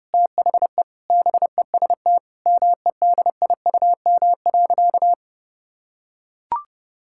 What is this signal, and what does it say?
Morse code, heard as a keyed tone, about 30 wpm, 700 Hz. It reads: THE BEST MEDIUM.
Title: Sentences from Top 1000 Words - Review Part 6 - 30wpm